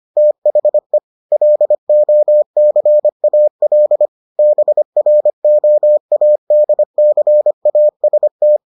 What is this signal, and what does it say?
Morse code, heard as a keyed tone, about 25 wpm, 600 Hz. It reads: THE LOCAL BROADCAST